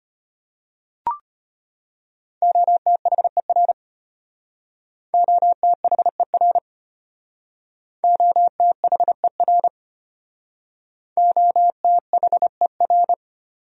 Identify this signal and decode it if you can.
Morse code, heard as a keyed tone, about 31 wpm, 700 Hz. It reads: OTHER OTHER OTHER OTHER
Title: Speed-Racing Top 300 Words 25wpm